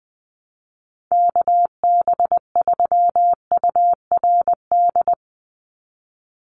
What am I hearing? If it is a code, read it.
Morse code, heard as a keyed tone, about 20 wpm, 700 Hz. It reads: KB3URD
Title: USA Call Signs 20wpm